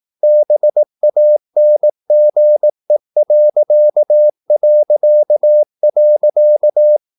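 Morse code, heard as a keyed tone, about 18 wpm, 600 Hz. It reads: BANGE...
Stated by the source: Slow (QRS) Morse practice at 18 wpm, CW med Gnister.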